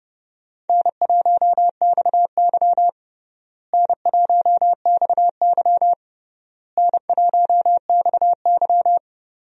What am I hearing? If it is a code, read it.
Morse code, heard as a keyed tone, about 29 wpm, 700 Hz. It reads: N1XY N1XY N1XY
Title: USA Call Signs - 4 Characters - Repeated 3x - 30wpm